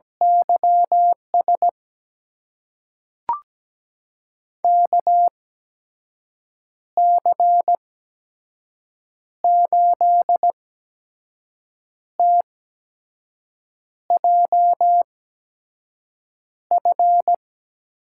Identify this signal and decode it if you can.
Morse code, heard as a keyed tone, about 17 wpm, spaced as for 5 wpm, 700 Hz. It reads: YS KC8TJF
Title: USA Call Signs - 6 Characters - 8x Character Spacing - 17wpm